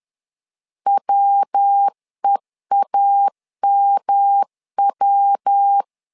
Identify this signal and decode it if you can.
Morse code, heard as a keyed tone, about 10 wpm, 800 Hz. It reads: WEAMW
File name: LEZIONI / 3 MWRO / 3 MW 10wpm 1959